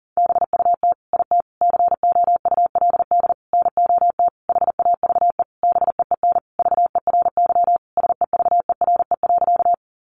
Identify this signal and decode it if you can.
Morse code, heard as a keyed tone, about 40 wpm, 700 Hz. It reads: BUT IT COULD NOT HAVE BEEN VERY SEVERE.
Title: Sentences from Top 1400 Words 40wpm